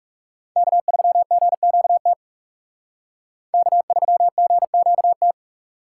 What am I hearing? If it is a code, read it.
Morse code, heard as a keyed tone, about 37 wpm, 700 Hz. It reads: K3GQT K3GQT
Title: Speed Racing - USA 1x3 Call Signs 30wpm